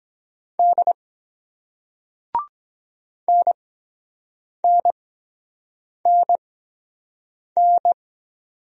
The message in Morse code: D NNNN

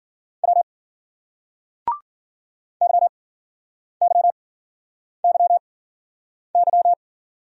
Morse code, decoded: W YYYY

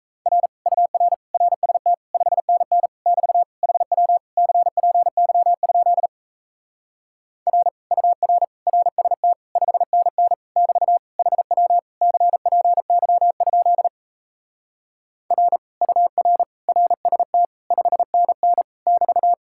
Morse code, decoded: R UR RST 5NN <BT> HW CPY? R UR RST 5NN <BT> HW CPY? R UR RST 5NN <BT>

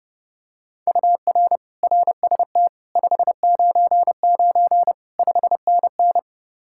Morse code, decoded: UR RST 599 5NN

